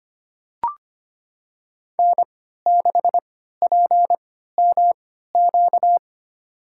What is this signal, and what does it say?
Morse code, heard as a keyed tone, about 25 wpm, spaced as for 14 wpm, 700 Hz. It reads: N6PMQ
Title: USA Call Signs - 5 Characters - 3x Character Spacing - 25wpm